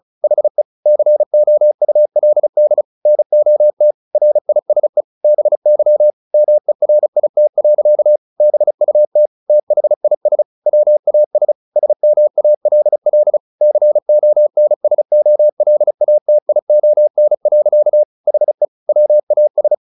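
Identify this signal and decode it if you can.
Morse code, heard as a keyed tone, about 34 wpm, 600 Hz. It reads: HE COULD NOT RISE BY MERIT. BUT THIS WAS SMALL CONSOLATION. HE WAS
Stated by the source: Morse practice at 35 wpm from CW med Gnister, sending Ken Follett's novel Never in English.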